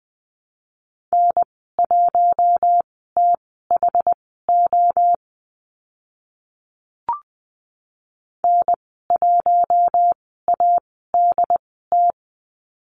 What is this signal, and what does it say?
Morse code, heard as a keyed tone, about 20 wpm, spaced as for 15 wpm, 700 Hz. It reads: N1THO N1ADT
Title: 069 Lesson 12g - Letter d - Call Signs 20wpm